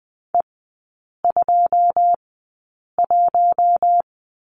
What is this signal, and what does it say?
Morse code, heard as a keyed tone, about 20 wpm, spaced as for 8 wpm, 690 Hz. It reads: E21
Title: 087 Lesson 15e - Number 5 - Three Characters 20wpm